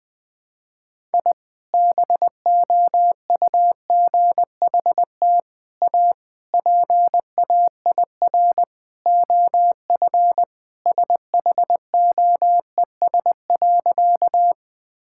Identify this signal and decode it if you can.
Morse code, heard as a keyed tone, about 20 wpm, 700 Hz. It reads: I BOUGHT A PAIR OF SHOES.
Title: Sentences with 3-Word Phrases - 20wpm